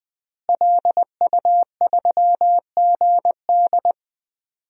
Morse code, heard as a keyed tone, about 20 wpm, 700 Hz. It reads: LU3GD
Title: DXCC-100 Call Signs - 5 Characters - Repeated 2x - 20wpm